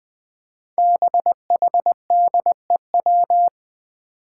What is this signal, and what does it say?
Morse code, heard as a keyed tone, about 21 wpm, 700 Hz. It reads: BHDEW
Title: Five Challenging Letter-Number Combinations 20wpm